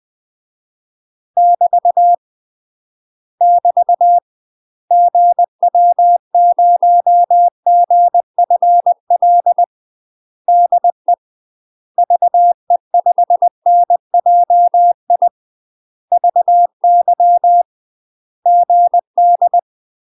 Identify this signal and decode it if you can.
Morse code, heard as a keyed tone, about 20 wpm, 700 Hz. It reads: <BT> <BT> GW0GFL DE VE5NJI VY GD
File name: UK MASTER QSOs TRAINING / 20WPM / UK MASTER QSOs 0050